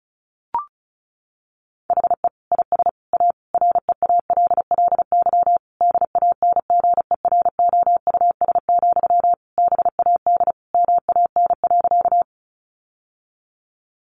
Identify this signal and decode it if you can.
Morse code, heard as a keyed tone, about 34 wpm, 700 Hz. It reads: HE IS A REALLY DANGEROUS, BAD MAN.
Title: Sentences from Top 1200 Words 35wpm